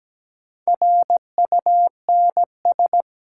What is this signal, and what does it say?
Morse code, heard as a keyed tone, about 17 wpm, 700 Hz. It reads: RUNS
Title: Sentences from Top 600 Words - New Word Drill - 17wpm